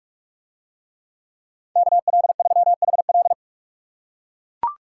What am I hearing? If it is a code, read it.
Morse code, heard as a keyed tone, about 40 wpm, 700 Hz. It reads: KL3HL